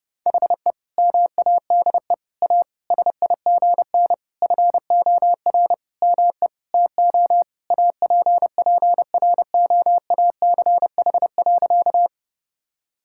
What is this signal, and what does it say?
Morse code, heard as a keyed tone, about 32 wpm, 700 Hz. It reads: HE MADE A SIGN FOR ME TO APPROACH.
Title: Sentences from Top 1000 Words - Review Part 6 - 30wpm